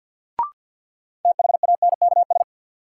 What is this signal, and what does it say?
Morse code, heard as a keyed tone, about 51 wpm, 700 Hz. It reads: THANKS
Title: Sentences from Top 600 Words - New Word Drill - 50wpm